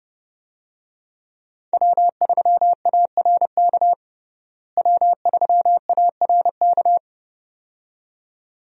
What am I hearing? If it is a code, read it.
Morse code, heard as a keyed tone, about 30 wpm, 700 Hz. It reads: W3ARK W3ARK
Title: USA Call Signs - 5 Characters - Repeated 2x - 30wpm